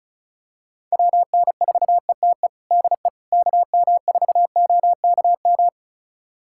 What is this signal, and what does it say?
Morse code, heard as a keyed tone, about 36 wpm, 700 Hz. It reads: WN4ETE DE KM4OKM
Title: USA Callsign DE Callsign 35wpm